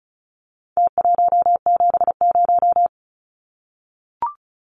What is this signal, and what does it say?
Morse code, heard as a keyed tone, about 34 wpm, 700 Hz. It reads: T170
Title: Numbers - Contest Style - 4-Digit 35wpm